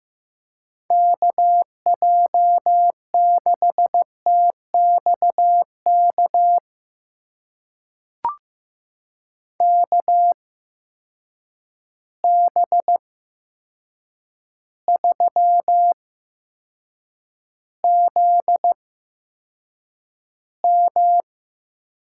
Morse code, heard as a keyed tone, about 15 wpm, spaced as for 7 wpm, 700 Hz. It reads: KJ6TXK KB3ZM
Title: USA Call Signs - 6 Characters - 8x Character Spacing - 15wpm